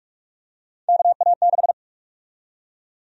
Morse code, heard as a keyed tone, about 44 wpm, 700 Hz. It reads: KA6
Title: USA Call Sign Prefixes - Repeated 2x - 45wpm